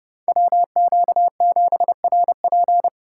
Morse code, heard as a keyed tone, about 36 wpm, 700 Hz. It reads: WQ7RP
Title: Speed Racing - USA 2x2 Call Signs 30wpm